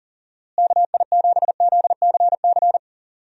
Morse code, heard as a keyed tone, about 37 wpm, 700 Hz. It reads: KI7ZCC
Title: USA Call Signs - 6 Characters - Repeated 2x - 40wpm